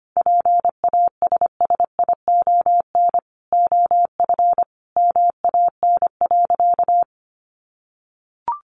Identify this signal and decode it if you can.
Morse code, heard as a keyed tone, about 25 wpm, 700 Hz. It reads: PASSION OF MAN.
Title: Sentences from Top 1400 Words 25wpm